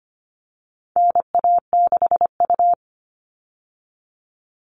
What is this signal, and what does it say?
Morse code, heard as a keyed tone, about 25 wpm, 700 Hz. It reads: NA6U
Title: All Cut Numbers - 4-Digit 25wpm